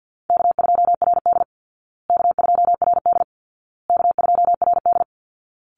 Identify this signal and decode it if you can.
Morse code, heard as a keyed tone, about 50 wpm, 700 Hz. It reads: K2RD K2RD K2RD
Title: Easy Call Signs Repeated 3x - 50wpm